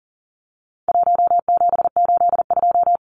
Morse code, heard as a keyed tone, about 40 wpm, 700 Hz. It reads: 1782
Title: Numbers - 4-Digit 40wpm